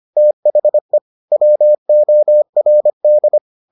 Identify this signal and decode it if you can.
Morse code, heard as a keyed tone, about 25 wpm, 600 Hz. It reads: THE WORD